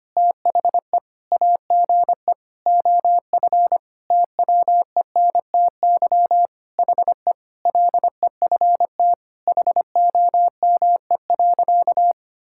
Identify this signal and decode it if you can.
Morse code, heard as a keyed tone, about 25 wpm, 700 Hz. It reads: THE AGE OF TWENTY HE LEFT HOME.
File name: practice / 25wpm / Sentences with 4-Word Phrases - 25wpm